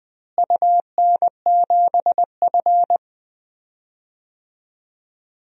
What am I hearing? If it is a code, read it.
Morse code, heard as a keyed tone, about 20 wpm, 700 Hz. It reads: UN7F
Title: DXCC-100 Call Signs - 4 Characters - Repeated 2x - 20wpm